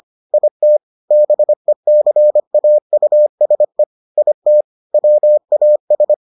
Morse code, heard as a keyed tone, about 25 wpm, 600 Hz. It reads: IT BECAUSE IT WAS